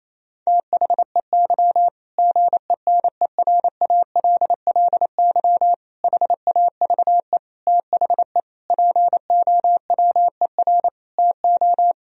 Morse code, heard as a keyed tone, about 27 wpm, 700 Hz. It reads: THEY GENERALLY HAVE THE POWER TO
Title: Sentences from Top 1000 Words - Review Part 4 - 28wpm